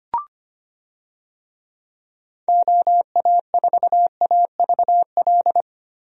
Morse code, heard as a keyed tone, about 25 wpm, 700 Hz. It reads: OA4AVL
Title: USA Call Signs - 6 Characters - Repeated 3x - 25wpm